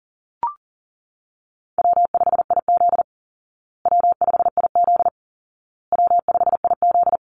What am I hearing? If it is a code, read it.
Morse code, heard as a keyed tone, about 40 wpm, 700 Hz. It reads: W5IZ W5IZ W5IZ
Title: Easy Call Signs Repeated 3x - 40wpm